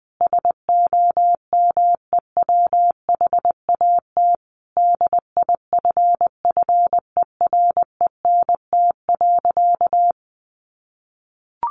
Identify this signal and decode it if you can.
Morse code, heard as a keyed tone, about 20 wpm, 700 Hz. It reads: SOMEWHAT DIFFERENT.
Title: Sentences from Top 1300 Words 20wpm